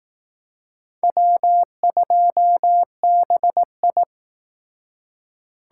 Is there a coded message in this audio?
Yes: Morse code from 1.0 to 4.0 s, heard as a keyed tone, about 20 wpm, 700 Hz. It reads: W2BI